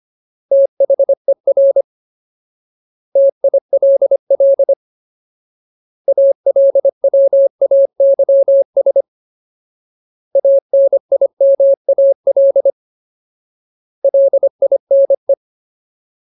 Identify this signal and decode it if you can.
Morse code, heard as a keyed tone, about 25 wpm, 550 Hz. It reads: THER TILL ALWAYS ANIMAL LINE
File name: Ultimate Word List / 25WPM Ultimate